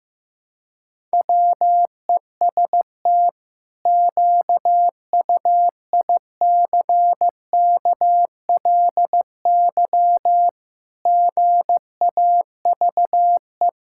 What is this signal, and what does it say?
Morse code, heard as a keyed tone, about 16 wpm, 700 Hz. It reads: WEST QUICKLY GAVE
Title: Pangram Challenge v2 - 15wpm